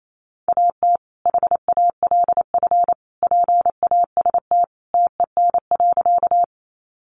Morse code, heard as a keyed tone, about 28 wpm, 700 Hz. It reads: AT HALF PAST TEN.
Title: Sentences from Top 600 Words 28wpm